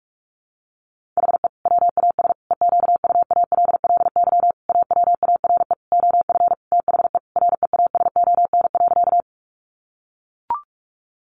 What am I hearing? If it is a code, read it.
Morse code, heard as a keyed tone, about 44 wpm, 700 Hz. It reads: HE WAS EQUALLY AWARE OF THE REASON.
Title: Sentences from Top 1400 Words 45wpm